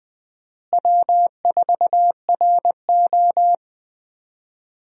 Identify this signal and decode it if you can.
Morse code, heard as a keyed tone, about 20 wpm, 700 Hz. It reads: W4RO
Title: USA Call Signs - 4 Characters - Repeated 2x - 20wpm